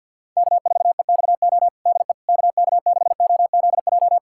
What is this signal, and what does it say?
Morse code, heard as a keyed tone, about 52 wpm, 700 Hz. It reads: K4EXO DE KK6OZJ
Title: USA Callsign DE Callsign 50wpm